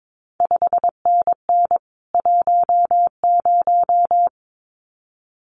Morse code, heard as a keyed tone, about 22 wpm, 700 Hz. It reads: 5NN 10
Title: CQ WW DX Contest Exchange Encore - 22wpm